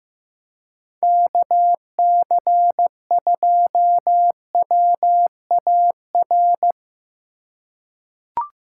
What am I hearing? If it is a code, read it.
Morse code, heard as a keyed tone, about 15 wpm, 700 Hz. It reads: KC2WAR
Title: USA Call Signs - 6 Characters - 2x Character Spacing - 15wpm